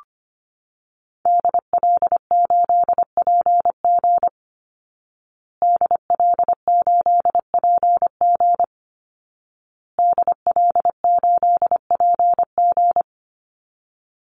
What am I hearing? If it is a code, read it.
Morse code, heard as a keyed tone, about 25 wpm, 700 Hz. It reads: DL8PG DL8PG DL8PG
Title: Call Signs Repeated 2x - 25wpm